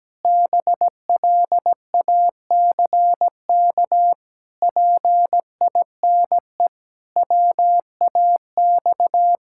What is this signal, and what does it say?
Morse code, heard as a keyed tone, about 17 wpm, 700 Hz. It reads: BLACK PINE WAX
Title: Pangram Challenge v2 - 17wpm